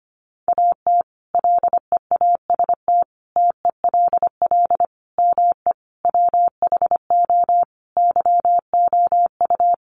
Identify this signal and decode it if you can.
Morse code, heard as a keyed tone, about 25 wpm, 700 Hz. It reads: AT LEAST TELL ME WHO YOU
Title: Sentences from Top 700 Words 25wpm